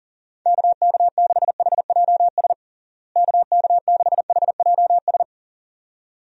Morse code, heard as a keyed tone, about 40 wpm, 700 Hz. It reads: KK6HJS KK6HJS
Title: USA Call Signs - 6 Characters - Repeated 2x - 40wpm